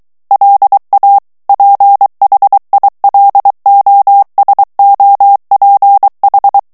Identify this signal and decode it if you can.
Morse code, heard as a keyed tone, about 23 wpm, 800 Hz. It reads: LA PHILOSOPH
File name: morse mp3 / FRANCAIS TEXTE CLAIR 2 VITESSE 22 MOTS MINUTES